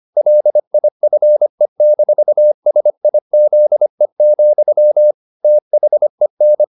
Morse code, heard as a keyed tone, about 25 wpm, 600 Hz. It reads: LIFE-SIZE, THEN